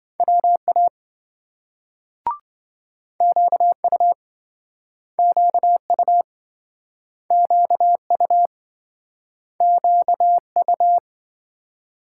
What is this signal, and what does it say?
Morse code, heard as a keyed tone, about 27 wpm, 700 Hz. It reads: WA QU QU QU QU